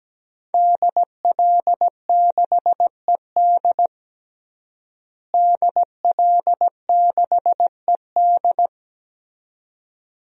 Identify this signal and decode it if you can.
Morse code, heard as a keyed tone, about 17 wpm, 700 Hz. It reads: DL6ED DL6ED
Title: DXCC-100 Call Signs - 5 Characters - Repeated 2x - 17wpm